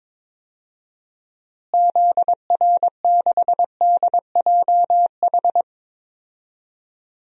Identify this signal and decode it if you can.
Morse code, heard as a keyed tone, about 22 wpm, 700 Hz. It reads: ZR6DJH
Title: DXCC-100 Call Signs - 6 Characters - Repeated 2x - 22wpm